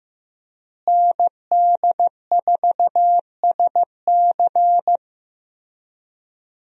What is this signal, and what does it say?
Morse code, heard as a keyed tone, about 15 wpm, 700 Hz. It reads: ND4SC
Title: USA Call Signs - 5 Characters - Repeated 2x - 15wpm